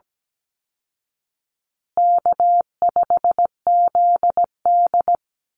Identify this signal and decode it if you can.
Morse code, heard as a keyed tone, about 17 wpm, 700 Hz. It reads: K5ZD
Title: Easy Call Signs Repeated 3x - 17wpm